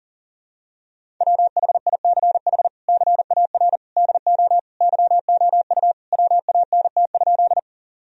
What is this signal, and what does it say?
Morse code, heard as a keyed tone, about 37 wpm, 700 Hz. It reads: WHICH CAR DO YOU WANT?
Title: Sentences from Top 1000 Words - Review Part 1 - 40wpm